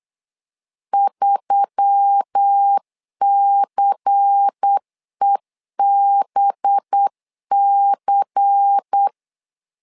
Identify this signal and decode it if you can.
Morse code, heard as a keyed tone, about 8 wpm, 800 Hz. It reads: 3CEBC
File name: LEZIONI / 5 LBVC / 5 LBVC 8wpm 2935